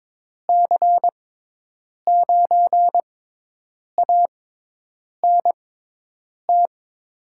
Morse code, heard as a keyed tone, about 22 wpm, spaced as for 8 wpm, 700 Hz. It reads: C9ANT
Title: USA Call Signs - 6 Characters - 6x Character Spacing - 22wpm